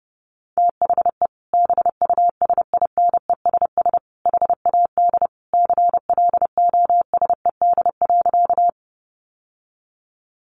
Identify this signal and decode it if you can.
Morse code, heard as a keyed tone, about 30 wpm, 700 Hz. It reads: THE BUSINESS HAD CLOSED.
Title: Sentences from Top 1400 Words 30wpm